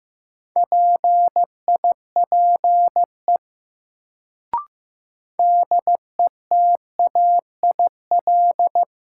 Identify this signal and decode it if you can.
Morse code, heard as a keyed tone, about 15 wpm, 700 Hz. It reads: PIPE DETAIL
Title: Sentences from Top 1500 Words - New Word Drill Alt - 15wpm